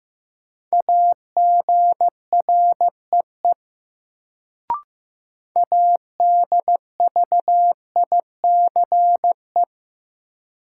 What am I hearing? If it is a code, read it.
Morse code, heard as a keyed tone, about 15 wpm, 700 Hz. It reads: AGREE ADVICE
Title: Sentences from Top 1100 Words - New Word Drill Alt - 15wpm